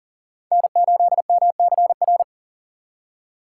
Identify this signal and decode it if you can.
Morse code, heard as a keyed tone, about 40 wpm, 700 Hz. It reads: N8MCR
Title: USA Call Signs - 5 Characters - 3x Character Spacing - 40wpm